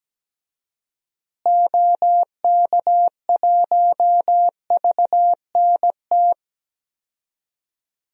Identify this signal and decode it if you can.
Morse code, heard as a keyed tone, about 17 wpm, 700 Hz. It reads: OK1VNT
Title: DXCC-100 Call Signs - 6 Characters - Repeated 2x - 17wpm